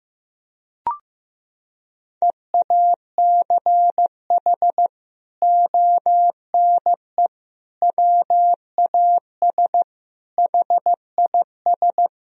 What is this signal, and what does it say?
Morse code, heard as a keyed tone, about 15 wpm, 700 Hz. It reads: EACH ONE WAS HIS
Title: Sentences from Top 1000 Words - Review Part 4 - 15wpm